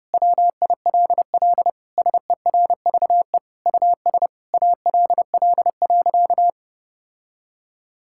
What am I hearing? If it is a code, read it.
Morse code, heard as a keyed tone, about 30 wpm, 700 Hz. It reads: WILL SERVE US ALL.
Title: Sentences from Top 1000 Words - Review Part 5 - 30wpm